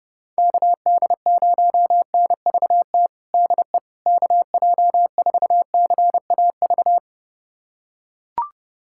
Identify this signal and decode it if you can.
Morse code, heard as a keyed tone, about 32 wpm, 700 Hz. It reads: KD0NVT DE KJ4CAV
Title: USA Callsign DE Callsign 30wpm